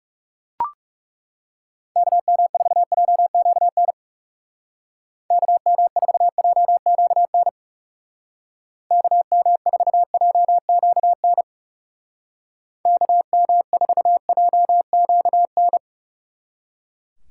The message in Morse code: KM4JQN KM4JQN KM4JQN KM4JQN

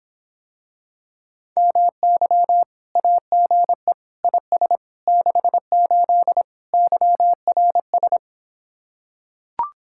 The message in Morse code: MY AGE IS 68 YRS

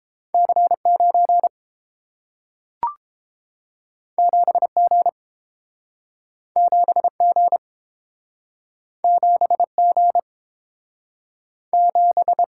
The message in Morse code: C 9 7 G 7 G 7 G 7

